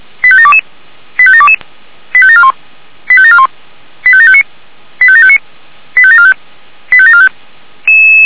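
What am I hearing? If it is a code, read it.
Five-tone selective call (ZVEI), heard as quick runs of five tones, tones of about 70 ms. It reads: 85722 85722 85741 85741 85759 85759 85745 85745
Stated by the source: ZVEI five-tone selective call 85729, received by SDR, archived by Feuerwehr Orlamünde (feuerwehr-orlamuende) fire brigade.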